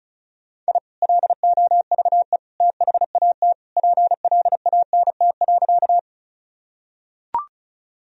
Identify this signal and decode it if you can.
Morse code, heard as a keyed tone, about 36 wpm, 700 Hz. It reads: I LOVE THAT PLANT.
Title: Sentences from Top 1000 Words - Review Part 1 - 35wpm